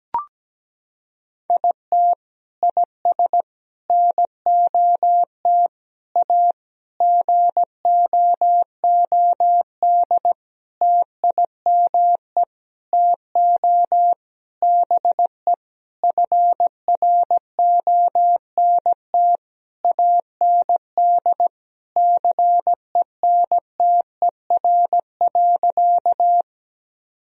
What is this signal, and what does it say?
Morse code, heard as a keyed tone, about 17 wpm, 700 Hz. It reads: IT IS NOT A GOOD TIME TO BE FRONT AND CENTER.
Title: Sentences from Top 1000 Words - Review Part 2 - 17wpm